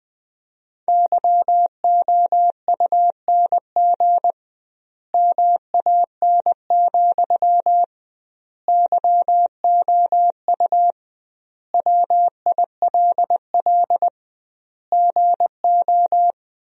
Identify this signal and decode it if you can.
Morse code, heard as a keyed tone, about 20 wpm, 700 Hz. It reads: YOUNG MAN, YOU WILL GO
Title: 2x Word-Spacing - Sentences from Top 400 Words - 20wpm